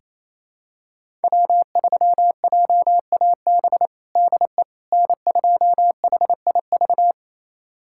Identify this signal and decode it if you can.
Morse code, heard as a keyed tone, about 28 wpm, 700 Hz. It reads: W3JAB DE N2HIV
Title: USA Callsign DE Callsign - 5 Characters - 28wpm